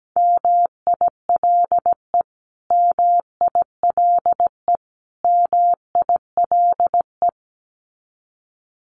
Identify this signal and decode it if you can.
Morse code, heard as a keyed tone, about 17 wpm, 700 Hz. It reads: MILE MILE MILE
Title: Top 500 Words Repeated 17wpm